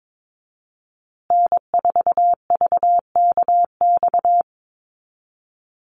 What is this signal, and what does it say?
Morse code, heard as a keyed tone, about 22 wpm, 700 Hz. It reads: N4VKX
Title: USA Call Signs Repeated 3x 22wpm